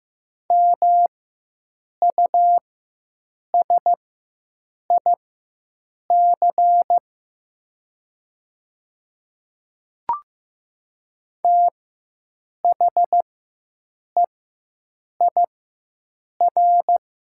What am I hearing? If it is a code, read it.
Morse code, heard as a keyed tone, about 15 wpm, spaced as for 7 wpm, 700 Hz. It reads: MUSIC THEIR
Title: 4x Character Spacing - Top 300 Words - 15wpm